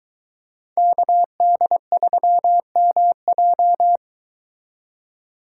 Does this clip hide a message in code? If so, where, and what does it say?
Yes: Morse code from 0.8 to 4.0 s, heard as a keyed tone, about 24 wpm, 700 Hz. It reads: KD3MJ